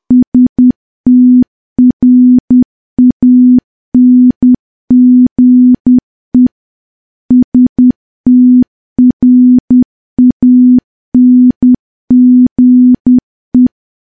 Morse code, heard as a keyed tone, about 10 wpm, 250 Hz. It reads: STRANGE STRANGE
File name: morse / morse-training-10WPM